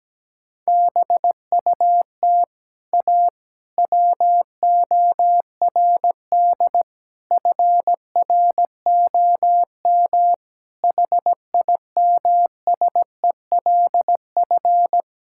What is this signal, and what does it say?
Morse code, heard as a keyed tone, about 17 wpm, 700 Hz. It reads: BUT A WORD FROM HIMSELF
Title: Sentences from Top 1600 Words 17wpm